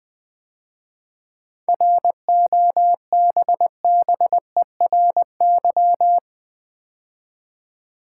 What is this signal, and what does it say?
Morse code, heard as a keyed tone, about 20 wpm, 700 Hz. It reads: ROBBERY
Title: 7 Letter Words - First 3 Letters Given - 20wpm